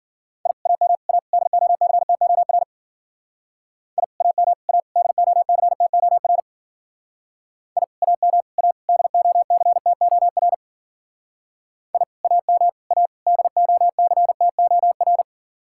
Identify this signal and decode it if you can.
Morse code, heard as a keyed tone, about 50 wpm, 700 Hz. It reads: I AM A DOCTOR I AM A DOCTOR I AM A DOCTOR I AM A DOCTOR
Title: Speed-Racing Common QSO Phrases - 40wpm